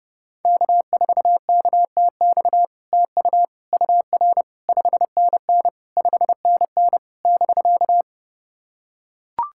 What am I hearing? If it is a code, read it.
Morse code, heard as a keyed tone, about 32 wpm, 700 Hz. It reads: K4KTX TU UR 5NN 5NN <BK>
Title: SOTA QSO Practice 30wpm